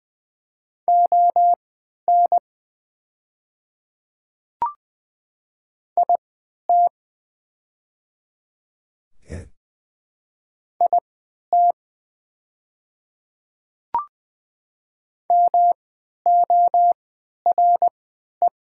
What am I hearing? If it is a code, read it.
Morse code, heard as a keyed tone, about 20 wpm, spaced as for 11 wpm, 700 Hz. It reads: ON IT IT MORE
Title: 3x Character Spacing - Top 100 Words - 20wpm